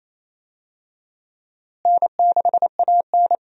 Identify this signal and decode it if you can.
Morse code, heard as a keyed tone, about 28 wpm, 700 Hz. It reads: N6AN
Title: USA Call Signs - 4 Characters - Repeated 3x - 28wpm